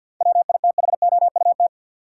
Morse code, heard as a keyed tone, about 48 wpm, 700 Hz. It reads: WITHOUT
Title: Sentences from Top 600 Words - New Word Drill - 50wpm